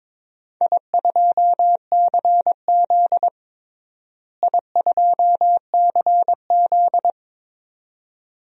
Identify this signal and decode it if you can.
Morse code, heard as a keyed tone, about 22 wpm, 700 Hz. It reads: I2CZ I2CZ
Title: DXCC-100 Call Signs - 4 Characters - Repeated 2x - 22wpm